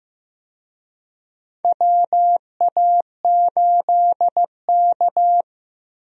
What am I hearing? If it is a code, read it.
Morse code, heard as a keyed tone, about 15 wpm, 700 Hz. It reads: WA8K
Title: USA Call Signs - 4 Characters - Repeated 2x - 15wpm